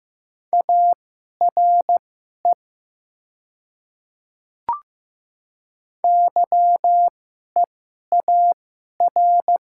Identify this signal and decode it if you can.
Morse code, heard as a keyed tone, about 15 wpm, spaced as for 11 wpm, 700 Hz. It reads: ARE YEAR
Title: 2x Character Spacing - Top 200 Words - 15wpm